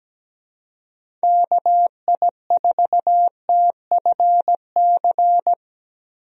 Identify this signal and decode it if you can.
Morse code, heard as a keyed tone, about 17 wpm, 700 Hz. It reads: KI4TFC
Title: USA Call Signs - 6 Characters - Repeated 3x - 17wpm